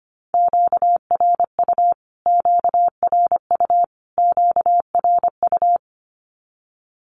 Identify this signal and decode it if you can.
Morse code, heard as a keyed tone, about 25 wpm, 700 Hz. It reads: QRU QRU QRU
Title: Q-Codes Repeated 3x 25wpm